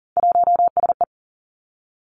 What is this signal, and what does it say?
Morse code, heard as a keyed tone, about 42 wpm, 700 Hz. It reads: 1SE
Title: Three Challenging Letter-Number Combinations 40wpm